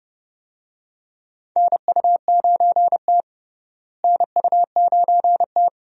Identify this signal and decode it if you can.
Morse code, heard as a keyed tone, about 30 wpm, 700 Hz. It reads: NU9T NU9T
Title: USA Call Signs - 4 Characters - Repeated 2x - 30wpm